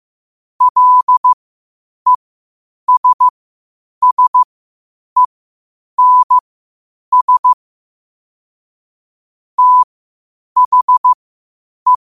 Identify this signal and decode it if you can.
Morse code, heard as a keyed tone, about 15 wpm, spaced as for 8 wpm, 1000 Hz. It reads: LESSENS THE
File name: qotdfiles / 8wpm